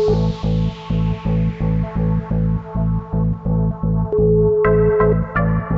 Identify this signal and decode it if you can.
Ringback tone (Europe, 425 Hz), with noise.